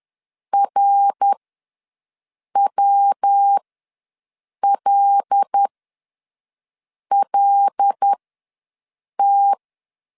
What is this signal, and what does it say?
Morse code, heard as a keyed tone, about 10 wpm, spaced as for 5 wpm, 800 Hz. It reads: RWLLT
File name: LEZIONI / 5 LBVC / 5 L ws 10wpm 3636